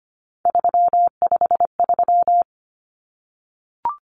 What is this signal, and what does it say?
Morse code, heard as a keyed tone, about 25 wpm, 700 Hz. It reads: 353
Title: Common Cut Numbers - 3-Digit 25wpm